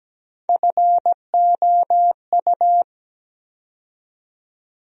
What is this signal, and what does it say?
Morse code, heard as a keyed tone, about 17 wpm, 700 Hz. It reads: FOU
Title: USA Call Sign Postfixes - 3 Letters 17wpm